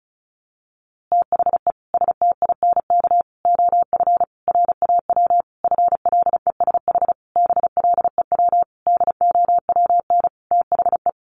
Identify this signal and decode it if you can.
Morse code, heard as a keyed tone, about 34 wpm, 700 Hz. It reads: THE STINK OF RAW FLESH BLEW DOWN THE
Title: A Sound Of Thunder 35wpm